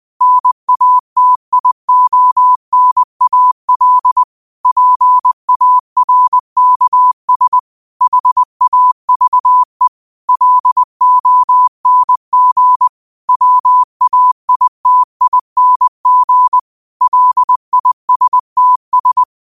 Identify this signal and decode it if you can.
Morse code, heard as a keyed tone, about 20 wpm, 1000 Hz. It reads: NATIONAL PARKS HAVE LONG WAITING LISTS
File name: qotdfiles / 20wpm